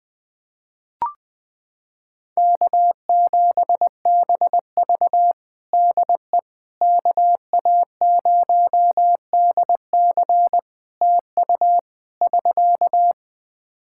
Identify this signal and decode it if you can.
Morse code, heard as a keyed tone, about 20 wpm, 700 Hz. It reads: K7BV DE KA0DC TU <SK>
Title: Practice QSO - Exchange 4 - Ending - 20wpm